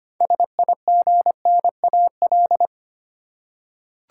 Morse code, heard as a keyed tone, about 25 wpm, 700 Hz. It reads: SIGNAL
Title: Six Letter Words - 25wpm